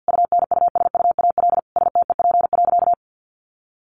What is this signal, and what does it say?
Morse code, heard as a keyed tone, about 50 wpm, 700 Hz. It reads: UNUSUAL STEP.
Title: Sentences from Top 1400 Words 50wpm